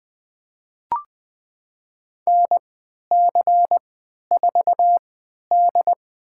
Morse code, heard as a keyed tone, about 20 wpm, spaced as for 11 wpm, 700 Hz. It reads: NC4D
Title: USA Call Signs - 4 Characters - 3x Character Spacing - 20wpm